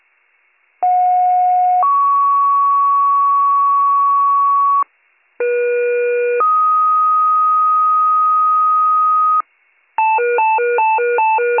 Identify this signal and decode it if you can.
Two-tone page, heard as a one-second tone, then a three-second tone, about 730 then 1120 Hz, about 480 then 1250 Hz. It is heard from 0.8 to 9.4 s.